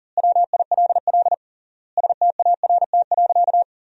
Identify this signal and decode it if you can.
Morse code, heard as a keyed tone, about 40 wpm, 700 Hz. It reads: WILL START.
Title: 3x Word-Spacing - Sentences from Top 300 Words - 40wpm